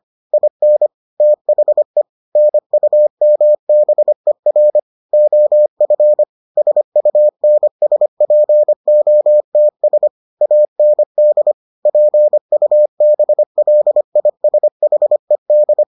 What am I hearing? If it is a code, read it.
Morse code, heard as a keyed tone, about 25 wpm, 600 Hz. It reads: IN THE NUMBER OF SUNSPOTS AND PUBLISHED